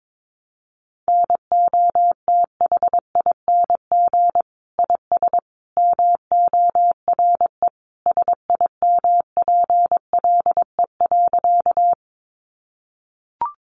Morse code, heard as a keyed tone, about 22 wpm, 700 Hz. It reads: NOTHING IS MORE SIMPLE.
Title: Sentences from Top 500 Words 22wpm